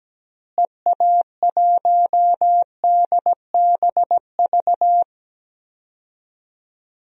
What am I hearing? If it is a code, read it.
Morse code, heard as a keyed tone, about 17 wpm, 700 Hz. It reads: EA1DBV
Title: DXCC-100 Call Signs - 6 Characters - Repeated 3x - 17wpm